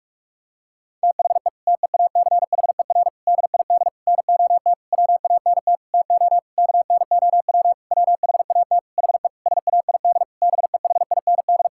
Morse code, heard as a keyed tone, about 46 wpm, 700 Hz. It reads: THE TEACHER DID NOT WANT TO KNOW WHAT HE SAID BEHIND
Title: Sentences from Top 1000 Words - Review Part 6 - 45wpm